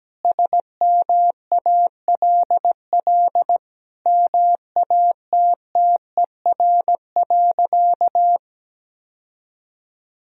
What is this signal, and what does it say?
Morse code, heard as a keyed tone, about 17 wpm, 700 Hz. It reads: SMALL MATTER.
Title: Sentences from Top 1000 Words - Review Part 1 - 17wpm